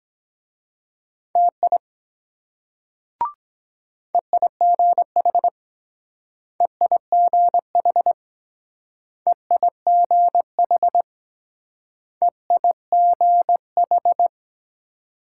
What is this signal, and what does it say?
Morse code, heard as a keyed tone, about 22 wpm, 700 Hz. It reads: TI EIGH EIGH EIGH EIGH